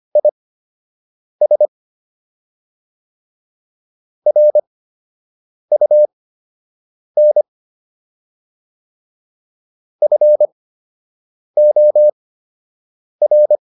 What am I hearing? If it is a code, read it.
Morse code, heard as a keyed tone, about 25 wpm, spaced as for 7 wpm, 600 Hz. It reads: IS RUN FOR